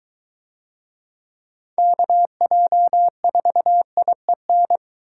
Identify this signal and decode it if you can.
Morse code, heard as a keyed tone, about 22 wpm, 700 Hz. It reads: KJ4IEN